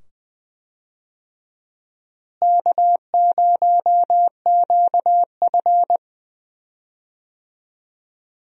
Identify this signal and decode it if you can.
Morse code, heard as a keyed tone, about 20 wpm, 700 Hz. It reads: K0QF
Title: USA Call Signs - 4 Characters - Repeated 2x - 20wpm